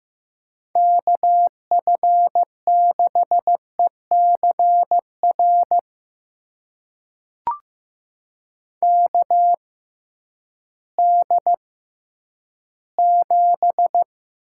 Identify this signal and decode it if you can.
Morse code, heard as a keyed tone, about 15 wpm, spaced as for 10 wpm, 700 Hz. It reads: KF6ECR KD7
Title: USA Call Signs - 6 Characters - 6x Character Spacing - 15wpm